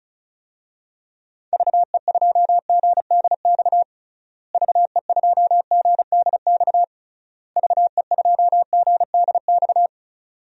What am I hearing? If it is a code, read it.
Morse code, heard as a keyed tone, about 36 wpm, 700 Hz. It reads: VE2GDX VE2GDX VE2GDX
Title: DXCC-100 Call Signs - 6 Characters - Repeated 3x - 35wpm